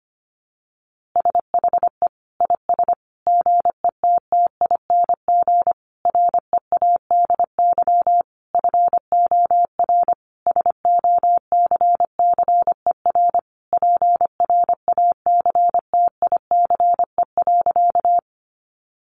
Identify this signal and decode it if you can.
Morse code, heard as a keyed tone, about 25 wpm, 700 Hz. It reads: SHE IS GETTING READY FOR SOCCER PRACTICE.
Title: Easy Phrases 25wpm